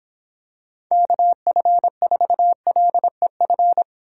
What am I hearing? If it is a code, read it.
Morse code, heard as a keyed tone, about 26 wpm, 700 Hz. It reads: KF4LEF
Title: Speed Racing - USA 2x3 Call Signs 22wpm